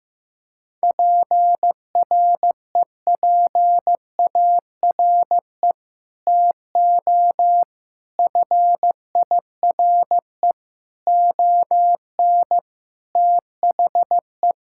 Morse code, heard as a keyed tone, about 15 wpm, 700 Hz. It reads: PREPARE TO FIRE ON THE
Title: Sentences from Top 1000 Words - Review Part 6 - 15wpm